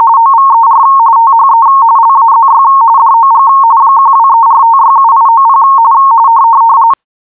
RTTY, about 45.45 baud. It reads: OVER THE LAZY DOG 1234567890
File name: PodcastGenerator / rtty-170-45